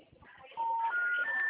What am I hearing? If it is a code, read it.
Special information tone, with noise.